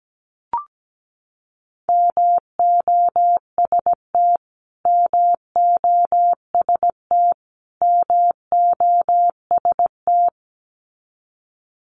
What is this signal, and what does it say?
Morse code, heard as a keyed tone, about 17 wpm, 700 Hz. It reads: MOST MOST MOST
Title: Top 200 Words Repeated 17wpm